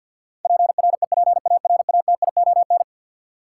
Morse code, heard as a keyed tone, about 47 wpm, 700 Hz. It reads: PREPARATION
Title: Sentences from Top 1400 Words - New Word Drill - 50wpm